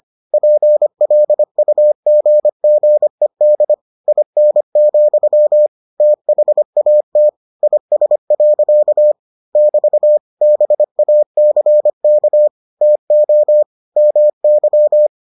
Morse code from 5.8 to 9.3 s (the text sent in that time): THAT IS.